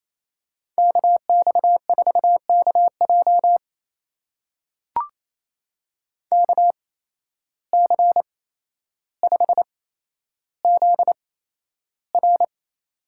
Morse code, heard as a keyed tone, about 28 wpm, spaced as for 12 wpm, 700 Hz. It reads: KX4KJ KC5ZR
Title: USA Call Signs - 5 Characters - 8x Character Spacing - 28wpm